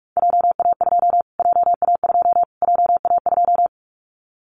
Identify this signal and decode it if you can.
Morse code, heard as a keyed tone, about 44 wpm, 700 Hz. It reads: JA2 JA2 JA2
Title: Callsign Prefixes for Top 25 DXCC - Repeated 3x - 45wpm